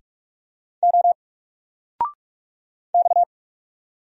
Morse code, heard as a keyed tone, about 44 wpm, 700 Hz. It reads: O X
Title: Single Character (Letter:Number:Punctuation:Prosign) 45wpm